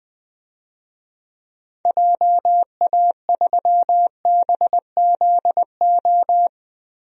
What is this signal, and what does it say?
Morse code, heard as a keyed tone, about 20 wpm, 700 Hz. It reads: JA3BZO
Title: DXCC-100 Call Signs - 6 Characters - 20wpm